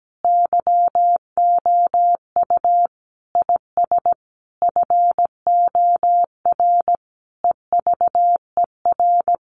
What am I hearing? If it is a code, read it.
Morse code, heard as a keyed tone, about 17 wpm, 700 Hz. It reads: YOU IS FOR EVER
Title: Sentences from Top 400 Words 17wpm